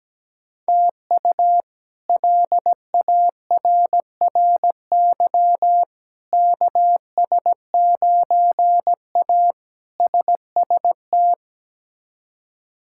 Morse code, heard as a keyed tone, about 17 wpm, 700 Hz. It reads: TU LARRY KS9A SST
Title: K1USN SST QSO Practice - 17wpm